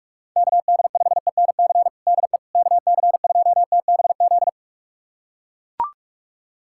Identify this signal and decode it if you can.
Morse code, heard as a keyed tone, about 42 wpm, 700 Hz. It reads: KD5ENC DE KC2TBZ
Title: USA Callsign DE Callsign - 6 Characters - 45wpm